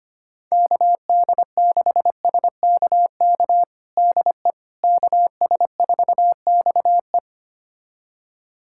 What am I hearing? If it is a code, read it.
Morse code, heard as a keyed tone, about 25 wpm, 700 Hz. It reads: KD6SKK DE KS4XE
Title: USA Callsign DE Callsign 25wpm